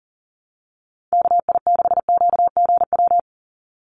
Morse code, heard as a keyed tone, about 40 wpm, 700 Hz. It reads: KI6QGW
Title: USA Call Signs 40wpm